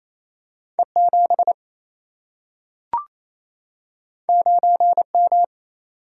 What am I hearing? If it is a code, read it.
Morse code, heard as a keyed tone, about 28 wpm, 700 Hz. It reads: E7 9M
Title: Single-Letter Single-Number 28wpm